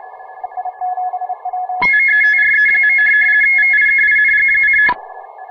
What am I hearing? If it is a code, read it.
RTTY, about 50 baud. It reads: METRE. VIKING:
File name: fr / DX RTTY